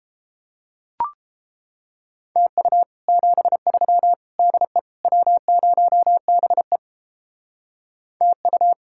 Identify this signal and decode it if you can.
Morse code, heard as a keyed tone, about 29 wpm, 700 Hz. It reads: TU 73 DE W0BE TU